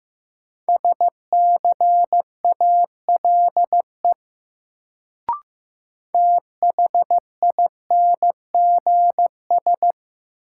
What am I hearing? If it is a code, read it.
Morse code, heard as a keyed tone, about 15 wpm, 700 Hz. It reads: SCALE THINGS
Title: Sentences from Top 1000 Words - Review Part 4 - New Word Drill - 15wpm